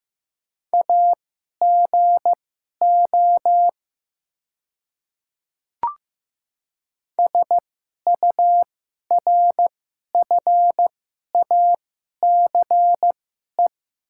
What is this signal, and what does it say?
Morse code, heard as a keyed tone, about 15 wpm, spaced as for 11 wpm, 700 Hz. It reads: AGO SURFACE
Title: 2x Character Spacing - Top 500 Words - 15wpm